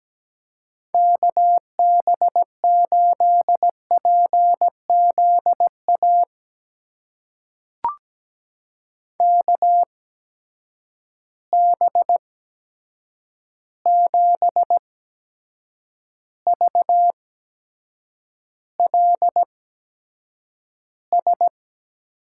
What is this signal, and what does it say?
Morse code, heard as a keyed tone, about 17 wpm, spaced as for 7 wpm, 700 Hz. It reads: KB8PZA KB7VLS
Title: USA Call Signs - 6 Characters - 8x Character Spacing - 17wpm